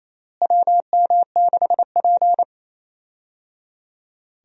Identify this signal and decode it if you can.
Morse code, heard as a keyed tone, about 28 wpm, 700 Hz. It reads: WM6P
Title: USA Call Signs - 4 Characters - Repeated 3x - 28wpm